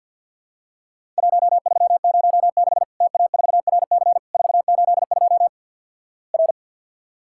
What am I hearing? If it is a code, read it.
Morse code, heard as a keyed tone, about 50 wpm, 700 Hz. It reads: 1306 TA4RC 482 R
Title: Worked All Europe DX Contest - QTC Exchange with Repeats - 50wpm